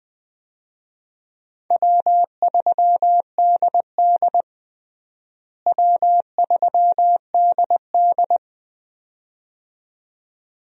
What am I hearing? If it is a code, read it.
Morse code, heard as a keyed tone, about 20 wpm, 700 Hz. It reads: W3DD W3DD
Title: USA Call Signs - 4 Characters - Repeated 2x - 20wpm